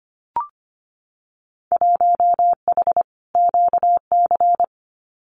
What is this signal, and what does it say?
Morse code, heard as a keyed tone, about 25 wpm, 700 Hz. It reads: 1H QC
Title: Winter Field Day Exchanges 25wpm